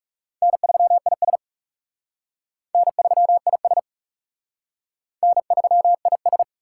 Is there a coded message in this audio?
Yes: Morse code from 0.4 to 6.4 s, heard as a keyed tone, about 39 wpm, 700 Hz. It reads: N3IS N3IS N3IS